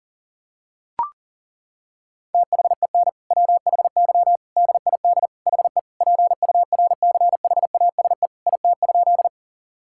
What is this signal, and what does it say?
Morse code, heard as a keyed tone, about 40 wpm, 700 Hz. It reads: THEN WHY DID HE PURCHASE IT?
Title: Sentences from Top 1000 Words - Review Part 5 - 40wpm